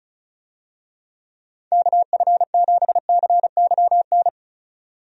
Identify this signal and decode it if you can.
Morse code, heard as a keyed tone, about 34 wpm, 700 Hz. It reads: KF7CYN
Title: USA Call Signs - 6 Characters - 35wpm